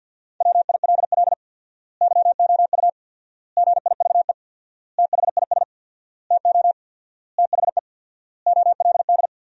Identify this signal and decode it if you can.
Morse code, heard as a keyed tone, about 47 wpm, 700 Hz. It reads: WILL YOU GIVE THIS TO THE OLD
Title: 4x Word-Spacing - Sentences from Top 200 Words - 50wpm